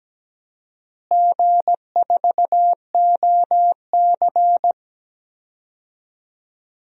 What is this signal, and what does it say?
Morse code, heard as a keyed tone, about 17 wpm, 700 Hz. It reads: G4OC
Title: DXCC-100 Call Signs - 4 Characters - Repeated 3x - 17wpm